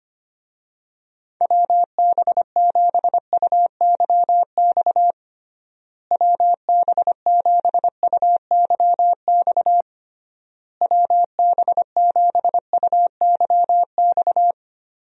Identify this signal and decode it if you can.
Morse code, heard as a keyed tone, about 25 wpm, 700 Hz. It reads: WB7UYX WB7UYX WB7UYX
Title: USA Call Signs - 6 Characters - Repeated 3x - 25wpm